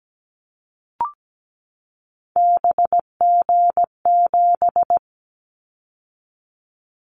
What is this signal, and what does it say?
Morse code, heard as a keyed tone, about 17 wpm, 700 Hz. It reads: BG7
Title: Callsign Prefixes for Top 100 DXCC - 17wpm